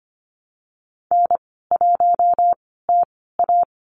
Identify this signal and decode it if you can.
Morse code, heard as a keyed tone, about 25 wpm, spaced as for 16 wpm, 700 Hz. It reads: N1TA
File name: Lesson 15g - Number 5 - Call Signs 25wpm